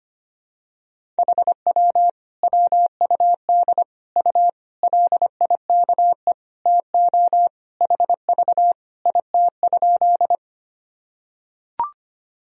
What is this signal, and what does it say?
Morse code, heard as a keyed tone, about 25 wpm, 700 Hz. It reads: HW WUD U LIKE TO HV IT?
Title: Sentences from Top 100 Words and Abbreviations 25wpm